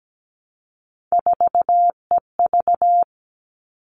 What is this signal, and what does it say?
Morse code, heard as a keyed tone, about 17 wpm, 700 Hz. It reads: 4EV
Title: All Cut Numbers - 3-Digit 17wpm